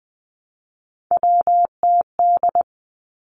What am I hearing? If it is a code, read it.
Morse code, heard as a keyed tone, about 20 wpm, 700 Hz. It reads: WTD
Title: Three Challenging Letter-Number Combinations 20wpm